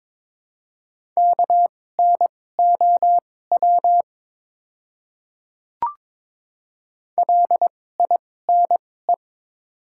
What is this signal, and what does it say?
Morse code, heard as a keyed tone, about 21 wpm, spaced as for 16 wpm, 700 Hz. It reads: KNOW LINE
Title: 2x Character Spacing - Top 200 Words - 22wpm